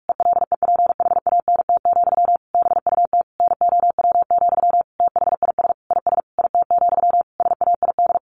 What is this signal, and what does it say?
Morse code, heard as a keyed tone, about 44 wpm, 700 Hz. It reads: ELEPHANT, BUT NOW, THIS IS IT, SAID
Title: A Sound Of Thunder 45wpm